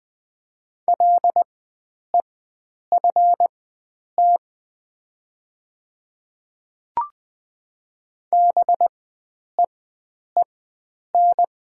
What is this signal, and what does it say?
Morse code, heard as a keyed tone, about 20 wpm, spaced as for 9 wpm, 700 Hz. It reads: LEFT BEEN
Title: 4x Character Spacing - Top 300 Words - 20wpm